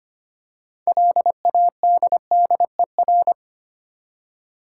Morse code, heard as a keyed tone, about 25 wpm, 700 Hz. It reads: LADDER
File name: practice / 25wpm / Difficult Words with Hints - 25wpm